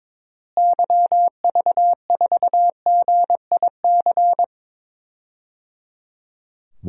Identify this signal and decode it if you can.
Morse code, heard as a keyed tone, about 22 wpm, 700 Hz. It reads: YV4GIC
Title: DXCC-100 Call Signs - 6 Characters - Repeated 3x - 22wpm